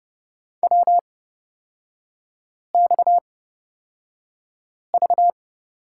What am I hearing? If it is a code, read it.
Morse code, heard as a keyed tone, about 30 wpm, spaced as for 5 wpm, 700 Hz. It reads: WXV
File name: Lesson 37e - Letter x - Three Characters 30wpm